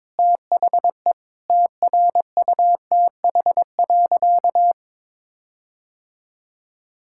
Morse code, heard as a keyed tone, about 21 wpm, 700 Hz. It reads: THE TRUTH.
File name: practice / 22wpm / Sentences with 8-Word Phrases - 22wpm